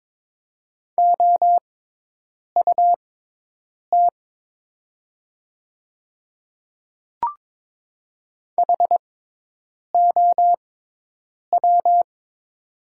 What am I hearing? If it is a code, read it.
Morse code, heard as a keyed tone, about 22 wpm, spaced as for 8 wpm, 690 Hz. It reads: OUT HOW
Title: 6x Character Spacing - Top 100 Words - 22wpm